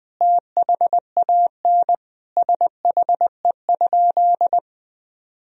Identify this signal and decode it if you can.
Morse code, heard as a keyed tone, about 20 wpm, 700 Hz. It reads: THAN SHE?
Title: Sentences from Top 1000 Words - Review Part 3 - 20wpm